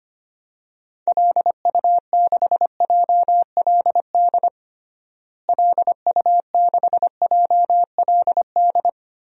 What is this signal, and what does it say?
Morse code, heard as a keyed tone, about 25 wpm, 700 Hz. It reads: LU6JLD LU6JLD
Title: DXCC-100 Call Signs - 6 Characters - Repeated 3x - 25wpm